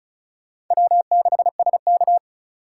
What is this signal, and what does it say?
Morse code, heard as a keyed tone, about 44 wpm, 700 Hz. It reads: W6SK